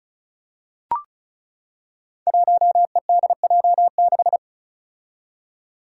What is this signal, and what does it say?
Morse code, heard as a keyed tone, about 36 wpm, 700 Hz. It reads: 1EDJ6
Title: Five Challenging Letter-Number Combinations 35wpm